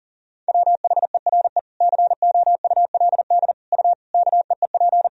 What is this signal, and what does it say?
Morse code, heard as a keyed tone, about 40 wpm, 700 Hz. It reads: WHERE COULD U KEEP
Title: Sentences from Top 300 Words and Abbreviations 40wpm